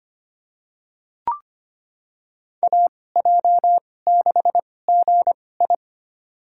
Morse code, heard as a keyed tone, about 25 wpm, spaced as for 18 wpm, 700 Hz. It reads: AJ6GI
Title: USA Call Signs - 5 Characters - 2x Character Spacing - 25wpm